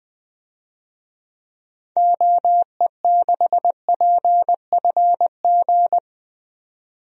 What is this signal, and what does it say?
Morse code, heard as a keyed tone, about 20 wpm, 700 Hz. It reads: OE6PFG
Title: DXCC-100 Call Signs - 6 Characters - 20wpm